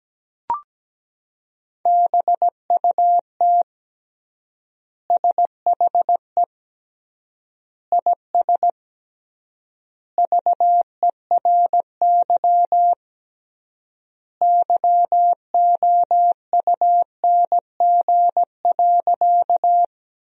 Morse code, heard as a keyed tone, about 17 wpm, 700 Hz. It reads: BUT SHE IS VERY YOUNG.
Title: 3x Word-Spacing - Sentences from Top 400 Words - 17wpm